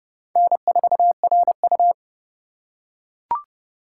Morse code, heard as a keyed tone, about 30 wpm, 700 Hz. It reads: N4RU